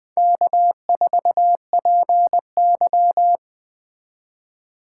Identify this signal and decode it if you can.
Morse code, heard as a keyed tone, about 20 wpm, 700 Hz. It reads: K4PY
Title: Speed Racing - USA 1x2 Call Signs 17wpm